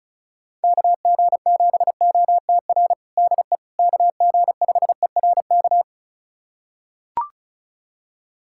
Morse code, heard as a keyed tone, about 34 wpm, 700 Hz. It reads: KG7OTR DE KG5ERK
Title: USA Callsign DE Callsign - 6 Characters - 35wpm